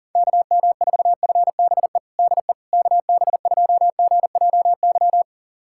Morse code, heard as a keyed tone, about 40 wpm, 700 Hz. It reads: KM4FBE DE KB2GJY
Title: USA Callsign DE Callsign - 6 Characters - 40wpm